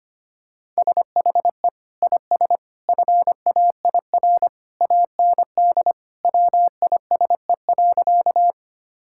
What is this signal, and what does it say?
Morse code, heard as a keyed tone, about 25 wpm, 700 Hz. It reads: SHE IS FAIR AND WISE.
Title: Sentences from Top 1600 Words 25wpm